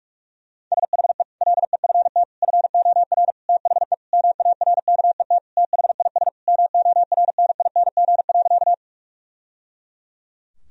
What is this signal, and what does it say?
Morse code, heard as a keyed tone, about 43 wpm, 700 Hz. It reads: SHE LEFT FOR THE MARKET THIS MORNING.
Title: Sentences from Top 1000 Words - Review Part 3 - 45wpm